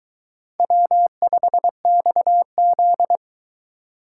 Morse code, heard as a keyed tone, about 22 wpm, 700 Hz. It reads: W5XZ